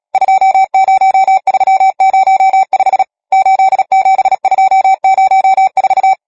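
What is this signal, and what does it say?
Morse code, heard as a keyed tone, about 37 wpm, 770 Hz. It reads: 20305 87204
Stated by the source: Morse code number practice, machine-sent training text (Nácvik telegrafie, OK1FCB).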